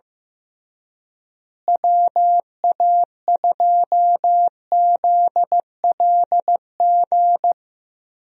Morse code, heard as a keyed tone, about 15 wpm, 700 Hz. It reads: WA2ZLG
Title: USA Call Signs - 6 Characters - Repeated 3x - 15wpm